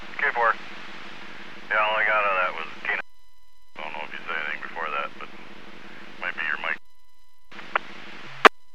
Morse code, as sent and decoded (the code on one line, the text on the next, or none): none